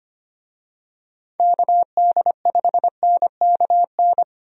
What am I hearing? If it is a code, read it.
Morse code, heard as a keyed tone, about 25 wpm, 700 Hz. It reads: KD5NKN
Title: USA Call Signs - 6 Characters - 25wpm